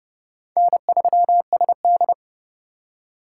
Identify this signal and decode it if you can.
Morse code, heard as a keyed tone, about 29 wpm, 700 Hz. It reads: N3SD
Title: USA Call Signs - 4 Characters - 2x Character Spacing - 30wpm